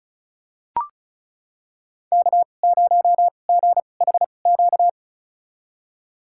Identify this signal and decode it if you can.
Morse code, heard as a keyed tone, about 36 wpm, spaced as for 25 wpm, 700 Hz. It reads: K0GHQ
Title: USA Call Signs - 5 Characters - 2x Character Spacing - 35wpm